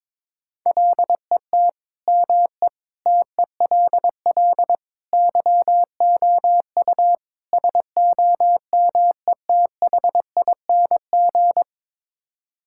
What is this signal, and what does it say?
Morse code, heard as a keyed tone, about 22 wpm, 700 Hz. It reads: LET ME TELL YOU SOMETHING
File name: practice / 22wpm / Build A Phrase - 5-Word Phrases - 22wpm